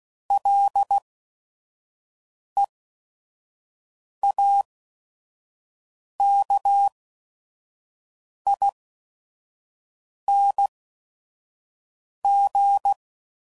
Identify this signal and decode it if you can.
Morse code, heard as a keyed tone, about 16 wpm, spaced as for 5 wpm, 800 Hz. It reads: LEAKING
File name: Suffixes / Special ING's / VerySlowToFaster ING's